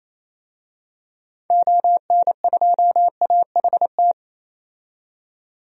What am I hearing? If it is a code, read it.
Morse code, heard as a keyed tone, about 28 wpm, 700 Hz. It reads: ON2AHT
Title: DXCC-100 Call Signs - 6 Characters - 28wpm